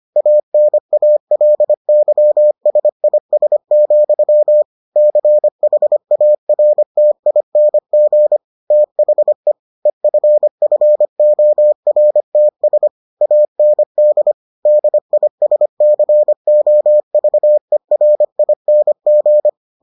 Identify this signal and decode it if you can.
Morse code, heard as a keyed tone, about 25 wpm, 600 Hz. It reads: ANALYSIS, CHARTING THE EFFORTS AND DISCOVERING